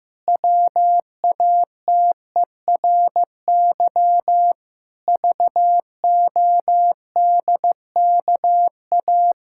Morse code, heard as a keyed tone, about 15 wpm, 700 Hz. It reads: WATERY VODKA
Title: Pangram Challenge v2 - 15wpm